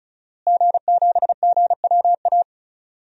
Morse code, heard as a keyed tone, about 36 wpm, 700 Hz. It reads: G7GWA
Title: DXCC-100 Call Signs - 5 Characters - Repeated 3x - 35wpm